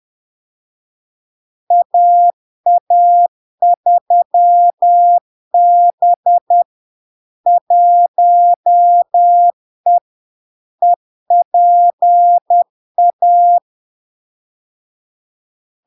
Morse code, heard as a keyed tone, about 10 wpm, 700 Hz. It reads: AA3B 1E EPA
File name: LARK-FD-2022-Log-in-CW-10wpm-700-Hz